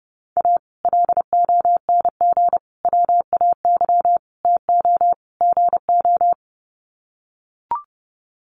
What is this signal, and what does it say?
Morse code, heard as a keyed tone, about 30 wpm, 700 Hz. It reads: A LONG WAY TO GO
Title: Sets of 5 Words 30wpm